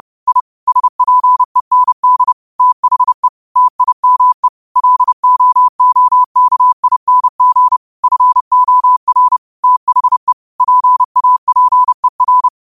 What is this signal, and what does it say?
Morse code, heard as a keyed tone, about 30 wpm, 1000 Hz. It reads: I SPEND THE TIME LOOKING FOR THE PAPER
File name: qotdfiles / 30wpm